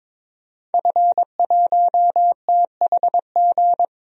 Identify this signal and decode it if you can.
Morse code, heard as a keyed tone, about 22 wpm, 700 Hz. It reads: F1THG
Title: DXCC-100 Call Signs - 5 Characters - Repeated 3x - 22wpm